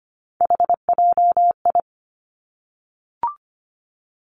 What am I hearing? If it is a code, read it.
Morse code, heard as a keyed tone, about 25 wpm, 700 Hz. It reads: HJI
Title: Three Challenging Letter-Number Combinations 25wpm